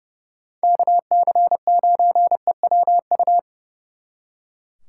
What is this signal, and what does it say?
Morse code, heard as a keyed tone, about 36 wpm, 700 Hz. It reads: KC9EWU
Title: Speed Racing - USA Call Signs 30wpm